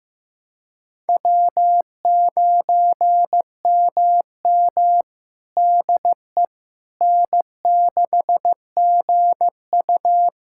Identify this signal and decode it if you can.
Morse code, heard as a keyed tone, about 15 wpm, 700 Hz. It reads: W9MM DE N6GU
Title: USA Callsign DE Callsign - 4 Characters - 15wpm